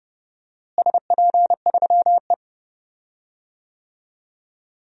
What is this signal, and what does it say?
Morse code, heard as a keyed tone, about 30 wpm, 700 Hz. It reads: SP3E
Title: DXCC-100 Call Signs - 4 Characters - Repeated 3x - 30wpm